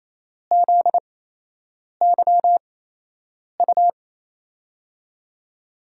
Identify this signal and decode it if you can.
Morse code, heard as a keyed tone, about 28 wpm, spaced as for 8 wpm, 700 Hz. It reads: ZYU
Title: USA Call Signs - 5 Characters - 8x Character Spacing - 28wpm